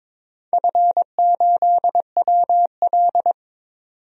Morse code, heard as a keyed tone, about 21 wpm, 700 Hz. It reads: F8WL
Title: DXCC-100 Call Signs - 4 Characters - 22wpm